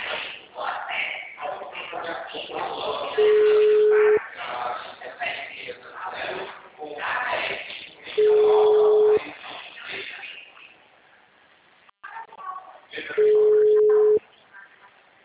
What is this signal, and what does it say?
Ringback tone (Europe, 425 Hz), with speech.